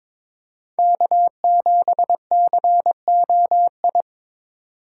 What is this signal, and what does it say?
Morse code, heard as a keyed tone, about 22 wpm, 700 Hz. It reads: K7COI
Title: USA Call Signs - 5 Characters - 8x Character Spacing - 22wpm